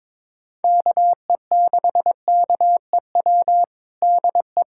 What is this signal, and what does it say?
Morse code, heard as a keyed tone, about 22 wpm, 700 Hz. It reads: KE6KEW DE
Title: USA Callsign DE Callsign 22wpm